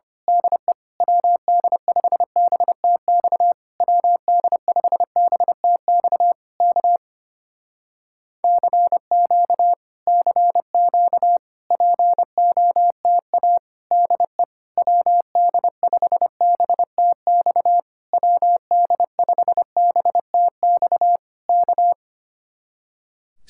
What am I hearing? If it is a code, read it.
Morse code, heard as a keyed tone, about 25 wpm, 700 Hz. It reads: DE WD5BTX WD5BTX K CQ CQ POTA DE WD5BTX WD5BTX K
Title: Speed-Racing Common QSO Phrases - 25wpm